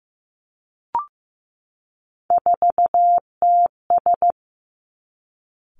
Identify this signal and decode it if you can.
Morse code, heard as a keyed tone, about 15 wpm, 700 Hz. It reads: 4TS
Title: Three Challenging Letter-Number Combinations 15wpm